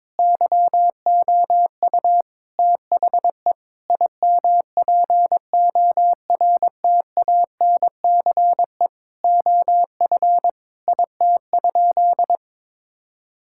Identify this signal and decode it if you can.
Morse code, heard as a keyed tone, about 22 wpm, 700 Hz. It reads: YOU THE IMPORTANCE OF IT?
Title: Sentences from Top 1600 Words 22wpm